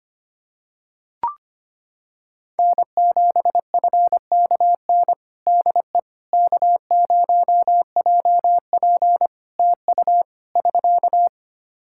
Morse code, heard as a keyed tone, about 25 wpm, 700 Hz. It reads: N7FKN DE K0JP TU <SK>
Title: Practice QSO - Exchange 4 - Ending - 25wpm